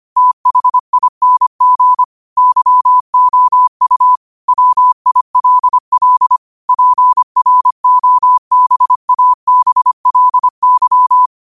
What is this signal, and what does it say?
Morse code, heard as a keyed tone, about 25 wpm, 1000 Hz. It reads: THING YOU WILL PROBABLY
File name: qotdfiles / 25wpm